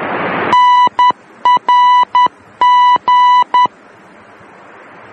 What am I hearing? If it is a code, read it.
Morse code, heard as a keyed tone, about 10 wpm, 990 Hz. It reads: NRG
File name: germany / mx / NRG-D